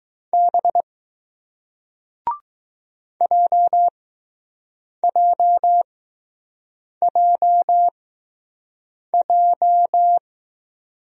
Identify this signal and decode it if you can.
Morse code, heard as a keyed tone, about 20 wpm, spaced as for 6 wpm, 700 Hz. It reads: B JJJJ